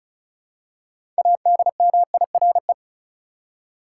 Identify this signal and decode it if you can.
Morse code, heard as a keyed tone, about 34 wpm, 700 Hz. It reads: ADMIRE
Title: Six Letter Words - 35wpm